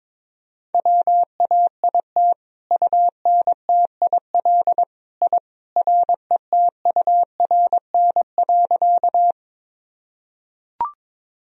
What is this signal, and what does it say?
Morse code, heard as a keyed tone, about 22 wpm, 700 Hz. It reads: WAIT UNTIL I RETURN.
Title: Sentences from Top 1000 Words - Review Part 1 - 22wpm